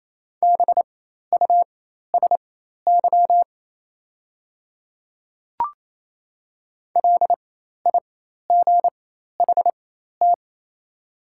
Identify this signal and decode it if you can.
Morse code, heard as a keyed tone, about 28 wpm, spaced as for 13 wpm, 700 Hz. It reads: BUSY LIGHT
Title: 4x Character Spacing - Top 500 Words - 28wpm